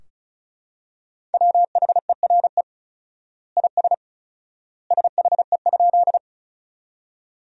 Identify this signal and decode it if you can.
Morse code, heard as a keyed tone, about 36 wpm, 700 Hz. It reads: WHERE IS SHE?
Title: 4x Word-Spacing - Sentences from Top 200 Words - 35wpm